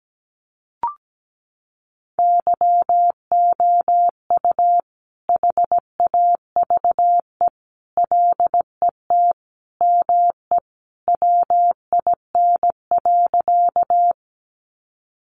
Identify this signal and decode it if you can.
Morse code, heard as a keyed tone, about 17 wpm, 700 Hz. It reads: YOU HAVE LET ME WIN.
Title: Sentences from Top 1500 Words 17wpm